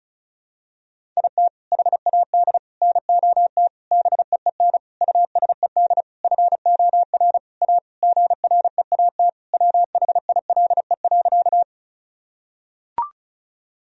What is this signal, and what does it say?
Morse code, heard as a keyed tone, about 34 wpm, 700 Hz. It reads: IT HAD NOT BEEN USED FOR A GREAT WHILE.
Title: Sentences from Top 1000 Words - Review Part 1 - 35wpm